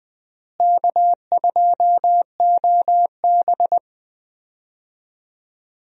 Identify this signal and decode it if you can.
Morse code, heard as a keyed tone, about 20 wpm, 700 Hz. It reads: K2OB
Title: USA Call Signs - 4 Characters - Repeated 2x - 20wpm